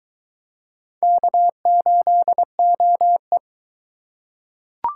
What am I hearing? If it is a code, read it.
Morse code, heard as a keyed tone, about 21 wpm, 700 Hz. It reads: K8OE